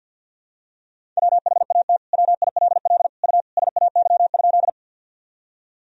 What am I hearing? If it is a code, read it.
Morse code, heard as a keyed tone, about 51 wpm, 700 Hz. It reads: WHAT WILL U SAY?
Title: Sentences from Top 200 Words and Abbreviations 50wpm